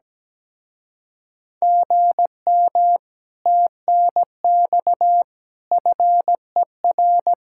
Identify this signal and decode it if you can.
Morse code, heard as a keyed tone, about 17 wpm, 700 Hz. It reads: GM TNX FER
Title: Speed-Racing Common QSO Phrases - 17wpm